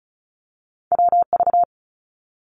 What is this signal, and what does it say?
Morse code, heard as a keyed tone, about 34 wpm, 700 Hz. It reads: WV
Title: US States Repeated 3x 35wpm